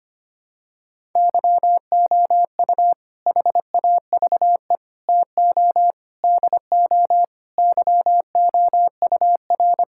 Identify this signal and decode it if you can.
Morse code, heard as a keyed tone, about 25 wpm, 700 Hz. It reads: YOU HAVE TO DO YOUR
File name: practice / 25wpm / Sentences with 4-Word Phrases - 25wpm